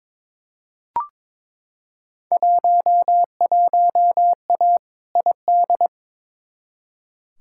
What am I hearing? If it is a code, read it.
Morse code, heard as a keyed tone, about 22 wpm, 700 Hz. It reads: 11A ID
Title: Field Day Exchanges Encore - 22wpm